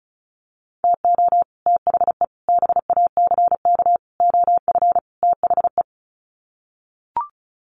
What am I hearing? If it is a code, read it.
Morse code, heard as a keyed tone, about 34 wpm, 700 Hz. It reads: TO THE BACK OF THE
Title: Sets of 5 Words 35wpm